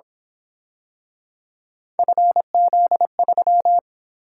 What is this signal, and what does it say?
Morse code, heard as a keyed tone, about 27 wpm, 700 Hz. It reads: FZ3
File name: Speed-Racing - Three Letter-Number Combinations - 22wpm